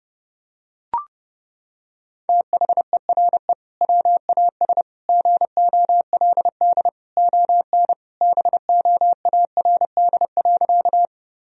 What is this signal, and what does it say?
Morse code, heard as a keyed tone, about 32 wpm, 700 Hz. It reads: THERE WAS GOLD ON BOARD.
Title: Sentences from Top 1000 Words - Review Part 4 - 30wpm